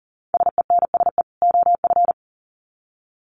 Morse code, heard as a keyed tone, about 38 wpm, 700 Hz. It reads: SENSE OF
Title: Sets of 2 Words 40wpm